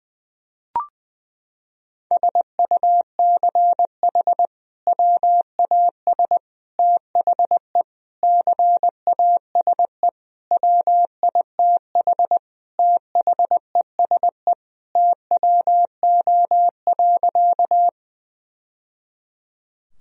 Morse code, heard as a keyed tone, about 21 wpm, 700 Hz. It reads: SUCH WAS THE CASE WITH THESE TWO.
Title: Sentences from Top 1000 Words - Review Part 1 - 20wpm